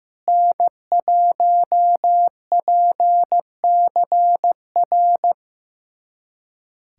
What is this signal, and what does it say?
Morse code, heard as a keyed tone, about 15 wpm, 700 Hz. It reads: N1PCR
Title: USA Callsign DE Callsign - 5 Characters - 15wpm